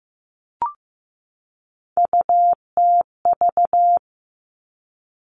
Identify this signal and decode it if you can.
Morse code, heard as a keyed tone, about 15 wpm, 700 Hz. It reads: UTV
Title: Three Challenging Letter-Number Combinations 15wpm